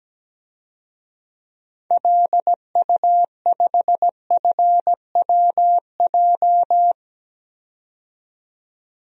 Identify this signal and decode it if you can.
Morse code, heard as a keyed tone, about 17 wpm, 700 Hz. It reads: LU5FWJ
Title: DXCC-100 Call Signs - 6 Characters - 17wpm